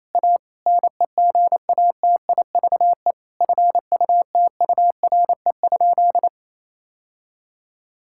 Morse code, heard as a keyed tone, about 27 wpm, 700 Hz. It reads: A NEGATIVE FUTURE?
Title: Sentences from Top 1000 Words - Review Part 6 - 28wpm